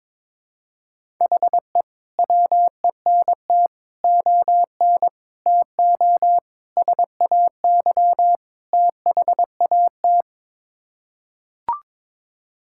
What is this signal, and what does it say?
Morse code, heard as a keyed tone, about 22 wpm, 700 Hz. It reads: HE WENT ON TO SAY THAT
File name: practice / 22wpm / Sets of 6 Words - 22wpm